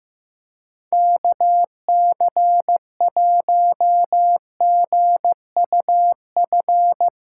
Morse code, heard as a keyed tone, about 16 wpm, 700 Hz. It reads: KC1GUF
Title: SOTA QSO Practice 15wpm